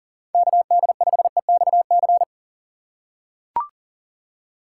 Morse code, heard as a keyed tone, about 40 wpm, 700 Hz. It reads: KD5EXC
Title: USA Call Signs - 6 Characters - 2x Character Spacing - 40wpm